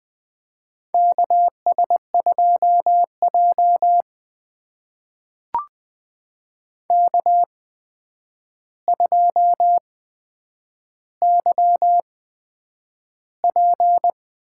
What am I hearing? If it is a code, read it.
Morse code, heard as a keyed tone, about 20 wpm, spaced as for 9 wpm, 700 Hz. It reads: KS2J K2YP
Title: USA Call Signs - 4 Characters - 8x Character Spacing - 20wpm